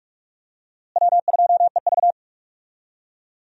Morse code, heard as a keyed tone, about 44 wpm, 700 Hz. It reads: W2EV
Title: USA Call Signs - 4 Characters - Repeated 2x - 45wpm